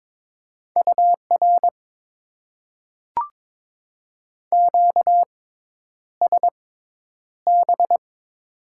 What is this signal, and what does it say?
Morse code, heard as a keyed tone, about 22 wpm, spaced as for 10 wpm, 700 Hz. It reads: UR QSB
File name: Top 90 QSO Elements - 6x Character Spacing - 22wpm